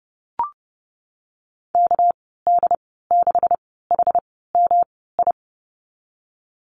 Morse code, heard as a keyed tone, about 30 wpm, spaced as for 17 wpm, 700 Hz. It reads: KD6HMI
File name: Lesson 37g - Letter x - Call Signs 30wpm